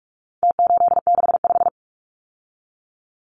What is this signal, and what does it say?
Morse code, heard as a keyed tone, about 44 wpm, 700 Hz. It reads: T865
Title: Numbers - Contest Style - 4-Digit 45wpm